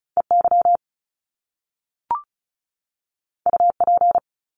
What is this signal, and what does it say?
Morse code, heard as a keyed tone, about 36 wpm, 700 Hz. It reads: EY UP
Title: Top Two Letter Combinations in Words 35wpm